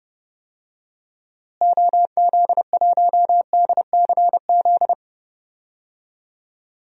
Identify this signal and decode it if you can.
Morse code, heard as a keyed tone, about 32 wpm, 700 Hz. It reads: OZ1DCZ
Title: DXCC-100 Call Signs - 6 Characters - Repeated 2x - 30wpm